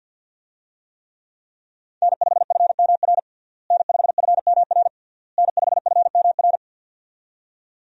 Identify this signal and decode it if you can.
Morse code, heard as a keyed tone, about 53 wpm, 700 Hz. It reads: N5FMR N5FMR N5FMR
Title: USA Call Signs - 5 Characters - Repeated 3x - 50wpm